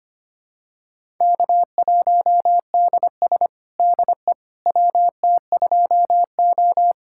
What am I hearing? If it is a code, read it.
Morse code, heard as a keyed tone, about 25 wpm, 700 Hz. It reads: K1DS DE WT2O
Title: USA Callsign DE Callsign - 4 Characters - 25wpm